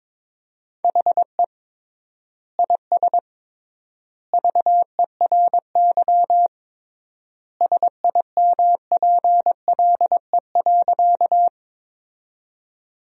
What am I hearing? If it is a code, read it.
Morse code, heard as a keyed tone, about 22 wpm, 700 Hz. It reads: HE IS VERY SIMPLE.
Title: 3x Word-Spacing - Sentences from Top 400 Words - 22wpm